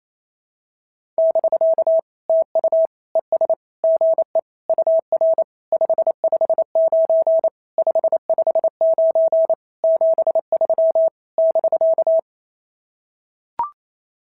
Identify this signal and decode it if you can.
Morse code, heard as a keyed tone, about 28 wpm, 650 Hz. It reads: <BK> TU ES GE UR 559 559 73 <BK>
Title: SOTA QSO Practice 28wpm